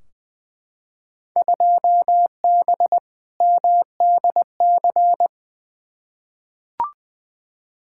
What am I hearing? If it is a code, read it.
Morse code, heard as a keyed tone, about 20 wpm, 700 Hz. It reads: 2B MDC
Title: Field Day Exchanges - 20wpm